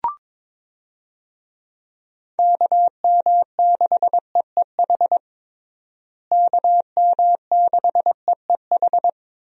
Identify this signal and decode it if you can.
Morse code, heard as a keyed tone, about 22 wpm, 700 Hz. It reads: KM6EEH KM6EEH
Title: USA Call Signs - 6 Characters - Repeated 2x - 22wpm